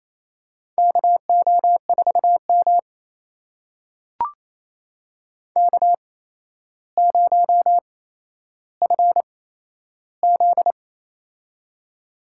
Morse code, heard as a keyed tone, about 28 wpm, spaced as for 12 wpm, 700 Hz. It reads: KO4M K0FZ
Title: USA Call Signs - 4 Characters - 8x Character Spacing - 28wpm